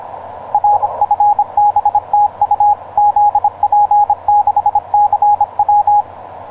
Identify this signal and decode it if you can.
Morse code, heard as a keyed tone, about 26 wpm, 830 Hz, with radio noise. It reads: R FB TU ZP6CW
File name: zp6cw ik1rgk etc 12m cw 19feb2010 1640